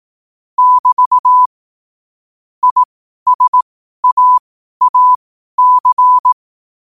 Morse code, heard as a keyed tone, about 18 wpm, spaced as for 12 wpm, 1000 Hz. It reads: <BT> ISAAC
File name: qotdfiles / 12wpm